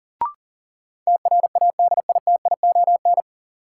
Morse code, heard as a keyed tone, about 40 wpm, 700 Hz. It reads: TRADITION
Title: Sentences from Top 1500 Words - New Word Drill Alt - 40wpm